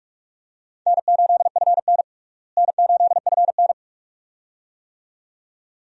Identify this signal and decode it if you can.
Morse code, heard as a keyed tone, about 46 wpm, 700 Hz. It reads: N8FN N8FN
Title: USA Call Signs - 4 Characters - Repeated 2x - 45wpm